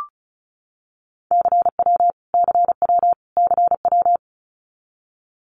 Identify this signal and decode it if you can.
Morse code, heard as a keyed tone, about 34 wpm, 700 Hz. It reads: CW CW CW
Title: Abbreviations Repeated 3x - 35wpm